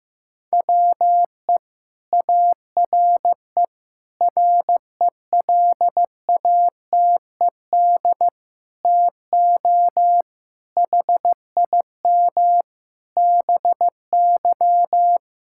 Morse code, heard as a keyed tone, about 16 wpm, 700 Hz. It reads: WE ARE RELATED TO HIM BY
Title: Sentences from Top 1000 Words - Review Part 5 - 15wpm